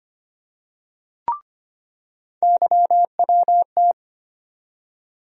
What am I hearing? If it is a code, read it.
Morse code, heard as a keyed tone, about 25 wpm, 700 Hz. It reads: YWT
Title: USA Call Sign Postfixes - 3 Letters 25wpm